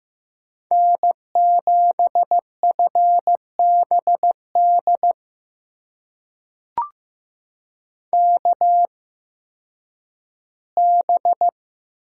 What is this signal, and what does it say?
Morse code, heard as a keyed tone, about 15 wpm, spaced as for 10 wpm, 700 Hz. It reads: N7FBD KB
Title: USA Call Signs - 5 Characters - 8x Character Spacing - 15wpm